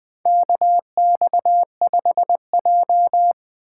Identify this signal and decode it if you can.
Morse code, heard as a keyed tone, about 20 wpm, 700 Hz. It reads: KX5J
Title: USA Call Signs - 6x Character Spacing - 20wpm